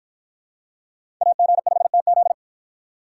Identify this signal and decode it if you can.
Morse code, heard as a keyed tone, about 53 wpm, 700 Hz. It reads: AG5TZ